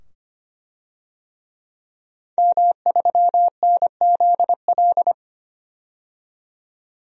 Morse code, heard as a keyed tone, about 25 wpm, 700 Hz. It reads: M3NZL
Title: DXCC-100 Call Signs - 5 Characters - 25wpm